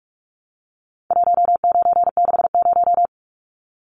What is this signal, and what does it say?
Morse code, heard as a keyed tone, about 45 wpm, 700 Hz. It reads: 1960
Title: Numbers - 4-Digit 45wpm